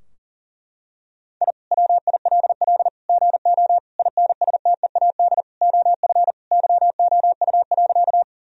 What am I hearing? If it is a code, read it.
Morse code, heard as a keyed tone, about 37 wpm, 700 Hz. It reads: I WILL GO INSTEAD OF YOU.
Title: Sentences from Top 1000 Words - Review Part 4 - 40wpm